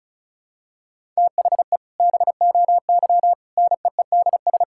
Morse code, heard as a keyed tone, about 36 wpm, 700 Hz. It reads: THE BOY NEEDS
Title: Sentences from Top 1600 Words 35wpm